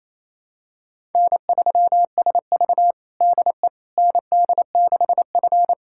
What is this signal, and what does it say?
Morse code, heard as a keyed tone, about 28 wpm, 700 Hz. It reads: N3SV DE ND6F
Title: USA Callsign DE Callsign - 4 Characters - 28wpm